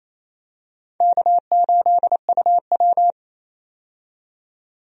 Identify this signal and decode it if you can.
Morse code, heard as a keyed tone, about 28 wpm, 700 Hz. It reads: K8UW
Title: USA Call Signs - 4 Characters - 28wpm